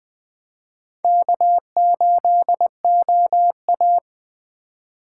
Morse code, heard as a keyed tone, about 21 wpm, 700 Hz. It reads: K8OA
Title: Speed Racing - USA 1x2 Call Signs 17wpm